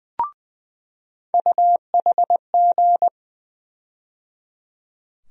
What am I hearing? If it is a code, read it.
Morse code, heard as a keyed tone, about 20 wpm, 700 Hz. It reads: UHG
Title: USA Call Sign Postfixes - 3 Letters 20wpm